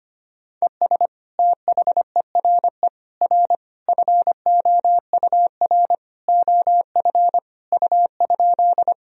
Morse code, heard as a keyed tone, about 25 wpm, 700 Hz. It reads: ES THERE R FOUR OF U?
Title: Sentences from Top 300 Words and Abbreviations 25wpm